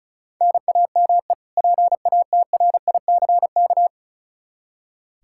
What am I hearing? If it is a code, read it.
Morse code, heard as a keyed tone, about 36 wpm, 700 Hz. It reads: NAME PATRICK
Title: Practice QSO - Exchange 1 - Beginning - 35wpm